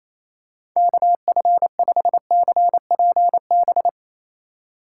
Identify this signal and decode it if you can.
Morse code, heard as a keyed tone, about 28 wpm, 690 Hz. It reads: KF5CPB
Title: USA Call Signs - 6 Characters - 2x Character Spacing - 28wpm